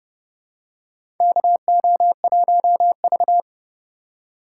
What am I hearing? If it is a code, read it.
Morse code, heard as a keyed tone, about 31 wpm, 700 Hz. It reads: KO1V
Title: USA Call Signs - 3x Character Spacing - 30wpm